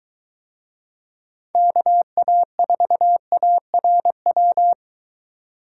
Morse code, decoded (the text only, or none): KA4ARW